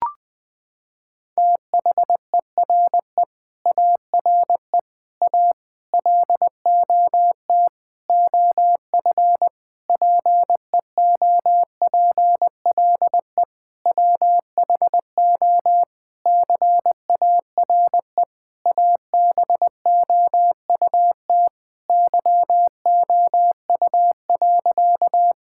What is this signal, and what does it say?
Morse code, heard as a keyed tone, about 20 wpm, 700 Hz. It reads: THERE ARE A LOT OF PEOPLE WHO CARE ABOUT YOU.
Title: Sentences with 7-Word Phrases - 20wpm